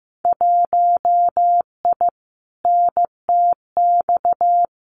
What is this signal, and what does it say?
Morse code, heard as a keyed tone, about 15 wpm, 700 Hz. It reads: 1I NTX
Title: Winter Field Day Exchanges 15wpm